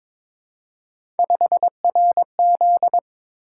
Morse code, heard as a keyed tone, about 22 wpm, 700 Hz. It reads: 5RZ
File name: practice / 22wpm / Three Letter-Number Combinations - 22wpm